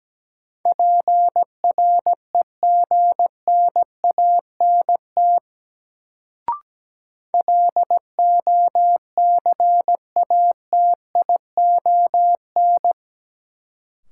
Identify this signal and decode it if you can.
Morse code, heard as a keyed tone, about 17 wpm, 700 Hz. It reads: PREGNANT LOCATION
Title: Sentences from Top 1000 Words - Review Part 6 - New Word Drill - 17wpm